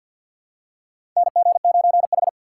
Morse code, heard as a keyed tone, about 47 wpm, 700 Hz. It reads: NG9H
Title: USA Call Signs - 4 Characters - 6x Character Spacing - 50wpm